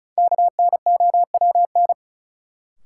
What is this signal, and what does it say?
Morse code, heard as a keyed tone, about 36 wpm, 700 Hz. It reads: KNOWN
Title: Sentences from Top 600 Words - New Word Drill - 35wpm